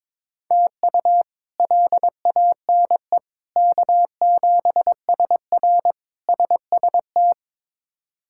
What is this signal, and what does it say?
Morse code, heard as a keyed tone, about 22 wpm, 700 Hz. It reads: TU LANE K7SR SST
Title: K1USN SST QSO Practice - 22wpm